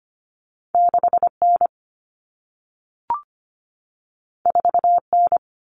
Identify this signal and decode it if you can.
Morse code, heard as a keyed tone, about 25 wpm, 700 Hz. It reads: 6N 4N
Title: Common Cut Numbers - 2-Digit 25wpm